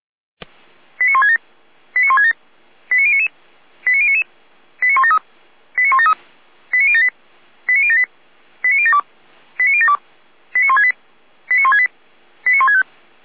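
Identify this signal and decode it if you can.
Five-tone selective call (ZVEI), heard as quick runs of five tones, tones of about 80 ms. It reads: 89167 89167 89099 89099 89173 89173 89078 89078 89072 89072 89168 89168 89165
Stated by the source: ZVEI five-tone selective call received by SDR, Feuerwehr Orlamünde.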